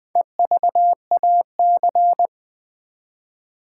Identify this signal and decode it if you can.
Morse code, heard as a keyed tone, about 20 wpm, 700 Hz. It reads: EVAC
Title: Build A Word - 4 Letters At A Time - 8-Letter Words - 20wpm